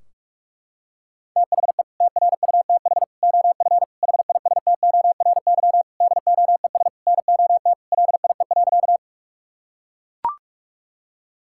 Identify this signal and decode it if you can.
Morse code, heard as a keyed tone, about 41 wpm, 700 Hz. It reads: THE TRUTH OF HISTORY DOES NOT LIE.
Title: Sentences from Top 1600 Words 45wpm